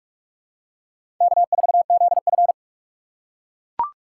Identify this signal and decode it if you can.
Morse code, heard as a keyed tone, about 43 wpm, 700 Hz. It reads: K4ZF
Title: USA Call Signs - 4 Characters - 6x Character Spacing - 45wpm